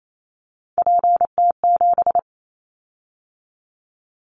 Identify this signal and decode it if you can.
Morse code, heard as a keyed tone, about 28 wpm, 700 Hz. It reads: PT7
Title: Callsign Prefixes for Top 50 DXCC - 28wpm